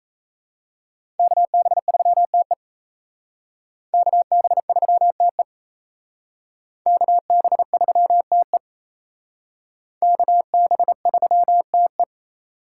Morse code, heard as a keyed tone, about 35 wpm, 700 Hz. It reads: KB3TE KB3TE KB3TE KB3TE
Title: Speed Racing - USA 2x2 Call Signs 28wpm